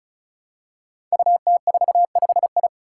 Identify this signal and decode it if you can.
Morse code, heard as a keyed tone, about 36 wpm, 700 Hz. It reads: UT45I
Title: Five Challenging Letter-Number Combinations 35wpm